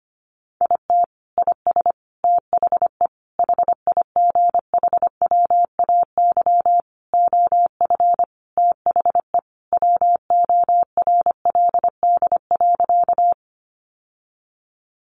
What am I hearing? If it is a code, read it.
Morse code, heard as a keyed tone, about 25 wpm, 700 Hz. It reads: IT IS THE HIGHWAY OF THE WORLD.
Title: Sentences from Top 1400 Words 25wpm